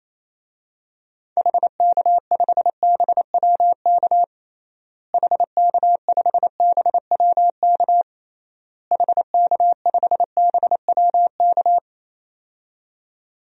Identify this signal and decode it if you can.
Morse code, heard as a keyed tone, about 28 wpm, 700 Hz. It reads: HK5BWK HK5BWK HK5BWK
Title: DXCC-100 Call Signs - 6 Characters - Repeated 3x - 28wpm